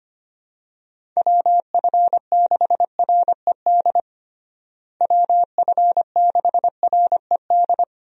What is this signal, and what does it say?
Morse code, heard as a keyed tone, about 25 wpm, 700 Hz. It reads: WF6RED WF6RED
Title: USA Call Signs - 6 Characters - Repeated 3x - 25wpm